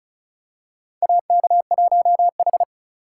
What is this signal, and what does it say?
Morse code, heard as a keyed tone, about 36 wpm, 700 Hz. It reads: AK1H
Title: USA Call Signs - 4 Characters - 3x Character Spacing - 35wpm